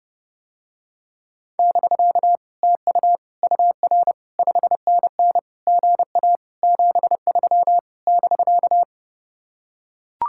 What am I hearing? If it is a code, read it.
Morse code, heard as a keyed tone, about 30 wpm, 700 Hz. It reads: <BK> TU UR 5NN GA 73 <BK>